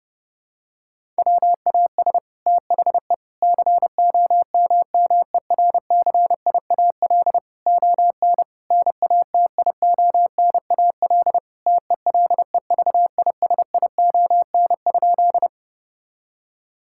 Morse code, heard as a keyed tone, about 30 wpm, 700 Hz. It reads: WAS THE COMMERCIAL ON NATIONAL TELEVISION?
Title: Sentences from Top 1000 Words - Review Part 5 - 30wpm